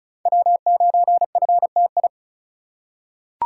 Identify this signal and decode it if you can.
Morse code, heard as a keyed tone, about 36 wpm, 700 Hz. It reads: W9FTI
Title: USA Call Signs - 5 Characters - 3x Character Spacing - 35wpm